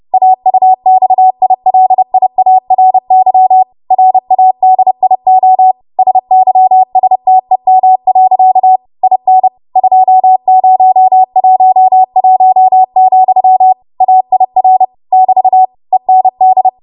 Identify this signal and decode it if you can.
Morse code, heard as a keyed tone, about 30 wpm, 750 Hz. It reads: AUXILIARY RADIO SYSTEM. IN 2011, AIR <BT> END
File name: Archive / 30 WPM / 30WPM